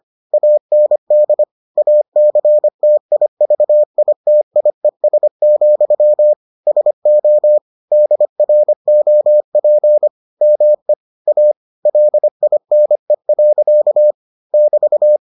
Morse code, heard as a keyed tone, about 25 wpm, 600 Hz. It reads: AND ACTIVITIES, SO DROP ME A LINE. <BT>